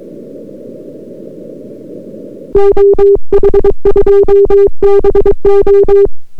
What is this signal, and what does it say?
Morse code, heard as a keyed tone, about 22 wpm, 400 Hz. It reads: OH2BO